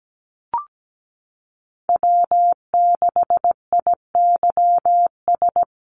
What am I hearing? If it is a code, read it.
Morse code, heard as a keyed tone, about 17 wpm, 700 Hz. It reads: W6IYS
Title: NA SOTA Chaser Callsigns 17wpm